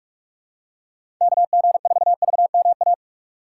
Morse code, heard as a keyed tone, about 46 wpm, 700 Hz. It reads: KG4VMA
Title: USA Call Signs - 6 Characters - 6x Character Spacing - 45wpm